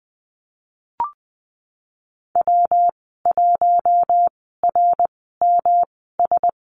Morse code, heard as a keyed tone, about 19 wpm, spaced as for 14 wpm, 700 Hz. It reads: W1RMS
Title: 128 Lesson 21g - Number 6 - Call Signs 20wpm